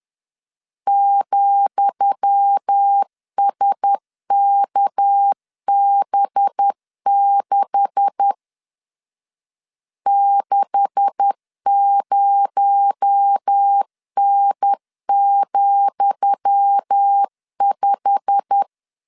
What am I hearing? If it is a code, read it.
Morse code, heard as a keyed tone, about 10 wpm, 800 Hz. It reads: ,SKB6 60N,5
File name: LEZIONI / 10 SIGNS / 10D COMA 10wpm 3013